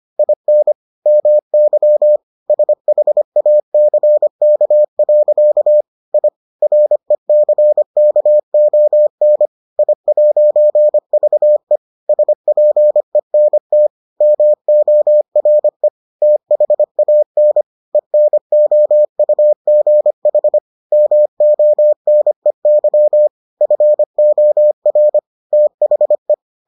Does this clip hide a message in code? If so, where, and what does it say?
Yes: Morse code from 0.2 to 26.3 s, heard as a keyed tone, about 25 wpm, 600 Hz. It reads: IN MY SHACK. I RECKON I'VE SPENT MORE THAN ENOUGH MONEY FOR THE